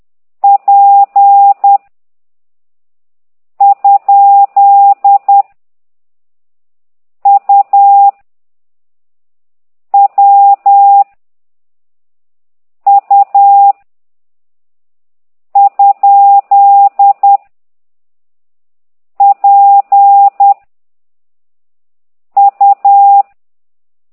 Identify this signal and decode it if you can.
Morse code, heard as a keyed tone, about 10 wpm, 800 Hz. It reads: P ? U W U ? P U
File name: ct2idw / exerc8